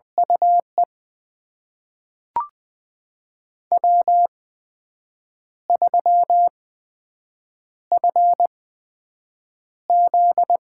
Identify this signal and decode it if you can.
Morse code, heard as a keyed tone, about 20 wpm, spaced as for 7 wpm, 700 Hz. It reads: UE W3FZ
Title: USA Call Signs - 4 Characters - 8x Character Spacing - 20wpm